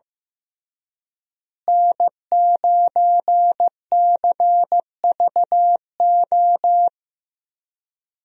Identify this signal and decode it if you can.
Morse code, heard as a keyed tone, about 15 wpm, 700 Hz. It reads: N9CVO
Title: USA Call Signs - 5 Characters - Repeated 3x - 15wpm